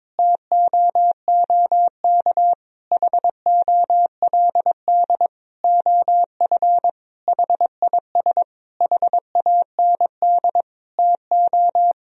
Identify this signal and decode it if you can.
Morse code, heard as a keyed tone, about 22 wpm, 700 Hz. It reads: TOOK HOLD OF HIS HAND TO
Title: Sentences from Top 1000 Words - Review Part 4 - 22wpm